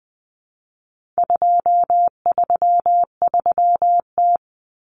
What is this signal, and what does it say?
Morse code, heard as a keyed tone, about 20 wpm, 700 Hz. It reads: 233T
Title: Common Cut Numbers - 4-Digit 20wpm